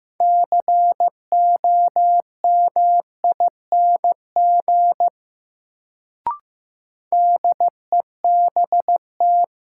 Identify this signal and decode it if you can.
Morse code, heard as a keyed tone, about 15 wpm, 700 Hz. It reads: COMING DEBT
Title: Sentences from Top 1300 Words - New Word Drill Alt - 15wpm